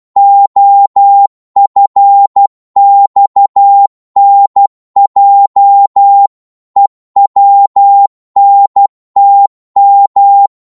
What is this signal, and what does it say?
Morse code, heard as a keyed tone, about 12 wpm, 800 Hz. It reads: OFXNJ EWNTM
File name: Morse / ltr-12wpm